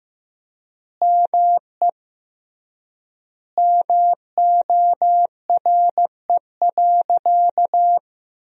Morse code, heard as a keyed tone, about 15 wpm, 700 Hz. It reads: ME MORE.
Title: 3x Word-Spacing - Sentences from Top 300 Words - 15wpm